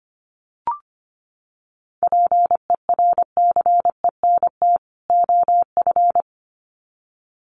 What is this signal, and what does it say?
Morse code, heard as a keyed tone, about 25 wpm, 700 Hz. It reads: PERCENT OF
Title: Sets of 2 Words 25wpm